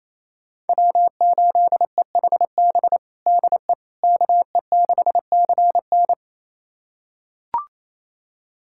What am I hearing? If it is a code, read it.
Morse code, heard as a keyed tone, about 28 wpm, 700 Hz. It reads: W8EHB DE KE6CN
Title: USA Callsign DE Callsign - 5 Characters - 28wpm